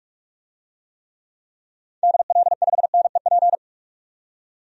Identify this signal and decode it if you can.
Morse code, heard as a keyed tone, about 47 wpm, 700 Hz. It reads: DL5NEP
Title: DXCC-100 Call Signs - 6 Characters - 45wpm